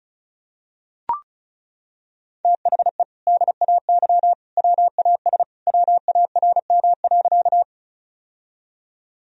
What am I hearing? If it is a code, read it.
Morse code, heard as a keyed tone, about 36 wpm, 700 Hz. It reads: THE DAY WAS WARM.
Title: Sentences from Top 1000 Words - Review Part 2 - 35wpm